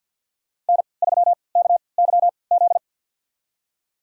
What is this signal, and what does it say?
Morse code, heard as a keyed tone, about 47 wpm, spaced as for 28 wpm, 690 Hz. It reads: N3KYZ
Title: USA Call Signs - 5 Characters - 3x Character Spacing - 50wpm